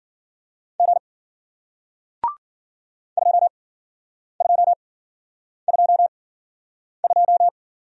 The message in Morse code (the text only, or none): G 2222